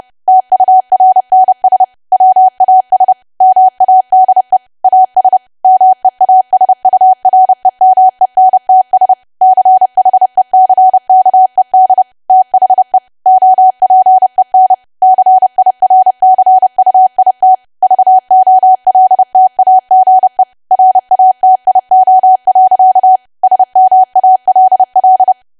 Morse code, heard as a keyed tone, about 30 wpm, 750 Hz. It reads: TURNS WAS MADE AS MEASUREMENTS CHECKED THE OPEN CIRCUIT VOLTAGE RATIO. SMALL
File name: cwa / W1AW-Level3 / 30WPM